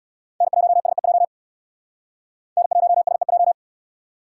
Morse code, heard as a keyed tone, about 77 wpm, 700 Hz. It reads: N1SEJ N1SEJ